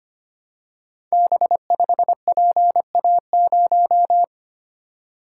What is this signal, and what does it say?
Morse code, heard as a keyed tone, about 25 wpm, 700 Hz. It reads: B5PA0
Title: Code Groups - 5 Letter:Numbers - 25wpm